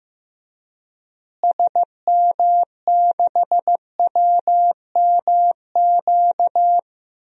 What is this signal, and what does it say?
Morse code, heard as a keyed tone, about 15 wpm, 700 Hz. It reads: SM6WMQ
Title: DXCC-100 Call Signs - 6 Characters - Repeated 3x - 15wpm